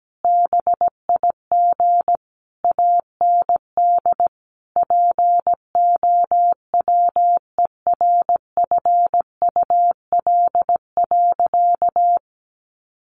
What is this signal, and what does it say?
Morse code, heard as a keyed tone, about 17 wpm, 700 Hz. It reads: BIG AND POWERFUL.
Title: Sentences from Top 1300 Words 17wpm